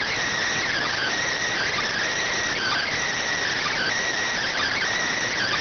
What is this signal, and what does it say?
Emergency Alert System (SAME) tones, heard as harsh buzzing bursts. It is heard from 0.7 to 4.8 s.